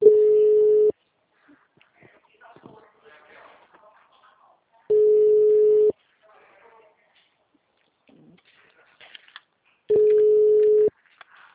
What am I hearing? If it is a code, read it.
Ringback tone (Europe, 425 Hz), with speech.